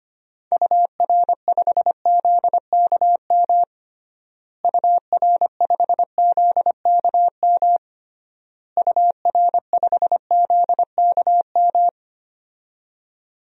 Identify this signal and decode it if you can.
Morse code, heard as a keyed tone, about 25 wpm, 700 Hz. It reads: UR5ZKM UR5ZKM UR5ZKM
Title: DXCC-100 Call Signs - 6 Characters - Repeated 3x - 25wpm